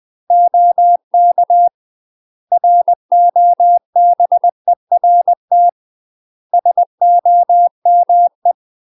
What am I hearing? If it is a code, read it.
Morse code, heard as a keyed tone, about 20 wpm, 700 Hz. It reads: OK ROBERT SOME
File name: UK MASTER QSOs TRAINING / 20WPM / UK MASTER QSOs 0066